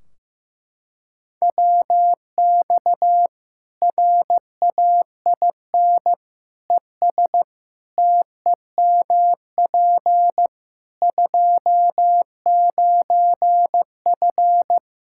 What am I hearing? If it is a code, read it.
Morse code, heard as a keyed tone, about 15 wpm, 700 Hz. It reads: WX RAIN ES TEMP 29F
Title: Practice QSO - Exchange 2 - Info - 15wpm